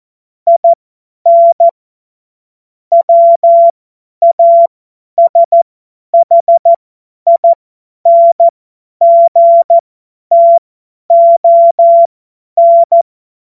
Morse code, heard as a keyed tone, about 14 wpm, spaced as for 10 wpm, 680 Hz. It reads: IN WASHINGTON
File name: Morsekurs / Witze / Witz 09. November 2021, 50 F70